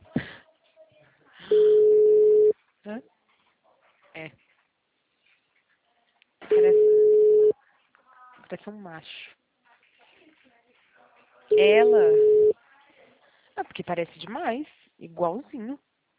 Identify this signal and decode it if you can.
Ringback tone (Europe, 425 Hz), with speech.
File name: out- -2002- - - .20746